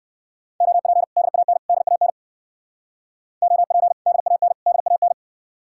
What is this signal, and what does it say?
Morse code, heard as a keyed tone, about 64 wpm, 700 Hz. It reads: OP DAN DAN OP DAN DAN